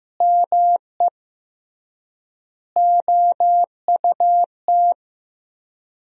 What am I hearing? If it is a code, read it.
Morse code, heard as a keyed tone, about 15 wpm, 700 Hz. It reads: ME OUT
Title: 3x Word-Spacing - Sentences from Top 400 Words - 15wpm